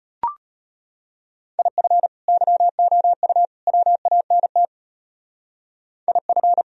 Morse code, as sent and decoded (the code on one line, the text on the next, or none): .. ..-. / -.-- --- ..- / .-- .- -. - / .. ..-.
IF YOU WANT IF